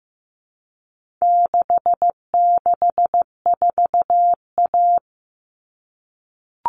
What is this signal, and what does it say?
Morse code, heard as a keyed tone, about 15 wpm, 700 Hz. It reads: 664A
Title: All Cut Numbers - 4-Digit 15wpm